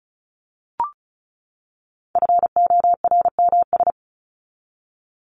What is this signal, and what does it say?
Morse code, heard as a keyed tone, about 34 wpm, 700 Hz. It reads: FORMS
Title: ES:S Suffix - Top 100 Words with Base Words 35wpm